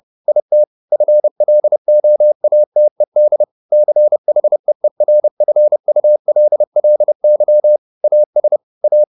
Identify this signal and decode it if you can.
Morse code, heard as a keyed tone, about 30 wpm, 600 Hz. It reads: IT FLOATED CHEERFULLY AS A
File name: the old man and the sea / the old man and the sea 1 30wpm